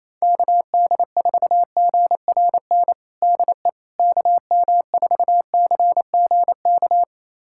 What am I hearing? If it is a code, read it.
Morse code, heard as a keyed tone, about 28 wpm, 700 Hz. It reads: KD4GRN DE KM4CGK
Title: USA Callsign DE Callsign - 6 Characters - 28wpm